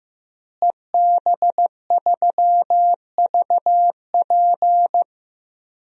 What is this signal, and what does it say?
Morse code, heard as a keyed tone, about 15 wpm, 700 Hz. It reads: EB3VP
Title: DXCC-100 Call Signs - 5 Characters - Repeated 3x - 15wpm